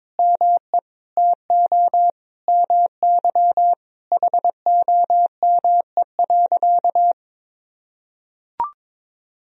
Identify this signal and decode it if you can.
Morse code, heard as a keyed tone, about 22 wpm, 700 Hz. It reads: ME TO MY HOME.
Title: Sentences from Top 1000 Words - Review Part 2 - 22wpm